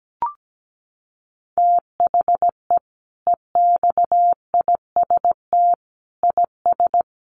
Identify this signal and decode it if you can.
Morse code, heard as a keyed tone, about 17 wpm, 700 Hz. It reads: THE EXIST IS
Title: Sentences from Top 1100 Words 17wpm